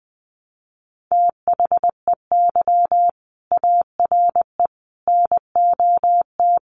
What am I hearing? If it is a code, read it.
Morse code, heard as a keyed tone, about 20 wpm, 700 Hz. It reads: THEY ARE NOT
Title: Sets of 5 Words Encore2 20wpm